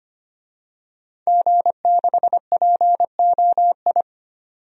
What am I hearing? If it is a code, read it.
Morse code, heard as a keyed tone, about 25 wpm, 700 Hz. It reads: G6POI
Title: DXCC-100 Call Signs - 5 Characters - 25wpm